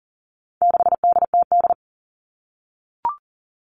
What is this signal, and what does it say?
Morse code, heard as a keyed tone, about 40 wpm, 700 Hz. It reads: 6DTD
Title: All Cut Numbers - 4-Digit 40wpm